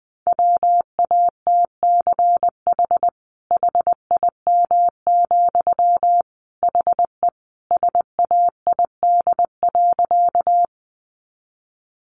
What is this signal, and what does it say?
Morse code, heard as a keyed tone, about 20 wpm, 700 Hz. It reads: WATCH HIM, HE SAID.
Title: Sentences from Top 300 Words 20wpm